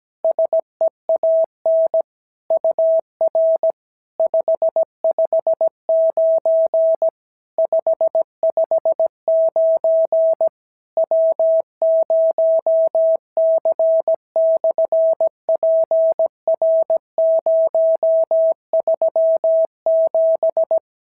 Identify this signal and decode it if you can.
Morse code, heard as a keyed tone, about 17 wpm, 650 Hz. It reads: SEAN UR 559 559 W0C/PR037
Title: SOTA QSO Practice 17wpm